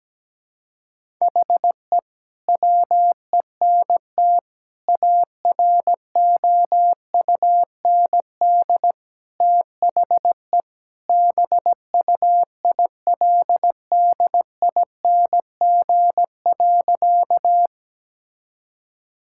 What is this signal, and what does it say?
Morse code, heard as a keyed tone, about 17 wpm, 700 Hz. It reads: HE WENT AROUND THE BUILDING.
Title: Sentences from Top 1000 Words - Review Part 5 - 17wpm